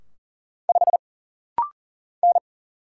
Morse code, heard as a keyed tone, about 40 wpm, 700 Hz. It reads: 5 N
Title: Single Letter-Number - Rapid-Fire 40wpm